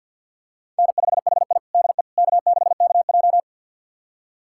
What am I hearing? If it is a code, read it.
Morse code, heard as a keyed tone, about 50 wpm, 700 Hz. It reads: N5HI DE K6KJ
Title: USA Callsign DE Callsign - 4 Characters - 50wpm